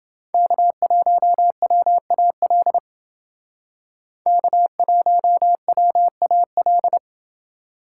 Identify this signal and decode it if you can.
Morse code, heard as a keyed tone, about 29 wpm, 700 Hz. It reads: K1WAL K1WAL